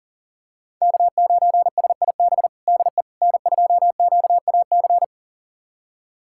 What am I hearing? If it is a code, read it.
Morse code, heard as a keyed tone, about 40 wpm, 700 Hz. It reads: K9SIB DE N2QAC
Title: USA Callsign DE Callsign - 5 Characters - 40wpm